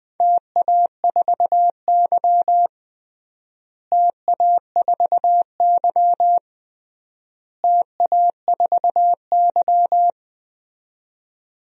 Morse code, heard as a keyed tone, about 21 wpm, 700 Hz. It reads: TA4Y TA4Y TA4Y
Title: DXCC-100 Call Signs - 4 Characters - Repeated 3x - 20wpm